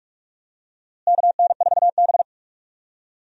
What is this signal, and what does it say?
Morse code, heard as a keyed tone, about 45 wpm, 700 Hz. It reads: KN4B
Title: USA Call Signs - 4 Characters - Repeated 3x - 45wpm